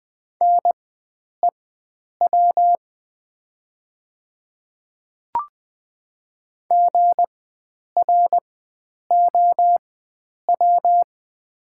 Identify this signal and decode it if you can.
Morse code, heard as a keyed tone, about 20 wpm, spaced as for 9 wpm, 700 Hz. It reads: NEW GROW
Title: 4x Character Spacing - Top 300 Words - 20wpm